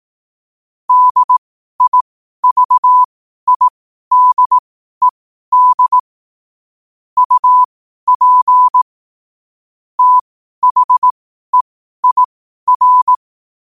Morse code, heard as a keyed tone, about 18 wpm, spaced as for 12 wpm, 1000 Hz. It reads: DIVIDED UP THEIR
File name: qotdfiles / 12wpm